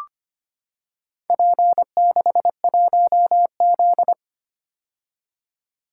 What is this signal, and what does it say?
Morse code, heard as a keyed tone, about 25 wpm, 700 Hz. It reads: P61Z
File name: practice / 25wpm / Four Letter-Number Combinations - 25wpm